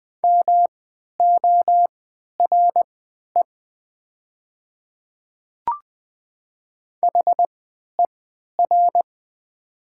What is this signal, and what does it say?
Morse code, heard as a keyed tone, about 20 wpm, spaced as for 11 wpm, 700 Hz. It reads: MORE HER
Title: 3x Character Spacing - Top 300 Words - 20wpm